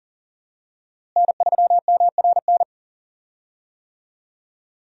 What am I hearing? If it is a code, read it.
Morse code, heard as a keyed tone, about 40 wpm, 700 Hz. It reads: N3MRN
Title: USA Call Signs - 5 Characters - Repeated 3x - 40wpm